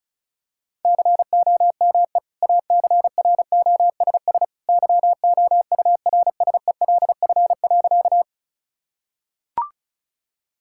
Morse code, heard as a keyed tone, about 36 wpm, 700 Hz. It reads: COME ACROSS YOURSELF.
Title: Sentences from Top 1000 Words - Review Part 5 - 35wpm